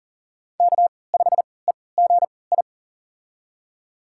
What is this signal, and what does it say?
Morse code, heard as a keyed tone, about 40 wpm, spaced as for 23 wpm, 700 Hz. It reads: K5EGI
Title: USA Call Signs - 5 Characters - 3x Character Spacing - 40wpm